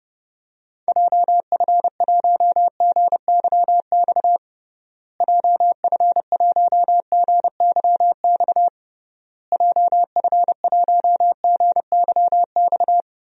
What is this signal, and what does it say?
Morse code, heard as a keyed tone, about 30 wpm, 700 Hz. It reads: JF1GYX JF1GYX JF1GYX
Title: DXCC-100 Call Signs - 6 Characters - Repeated 3x - 30wpm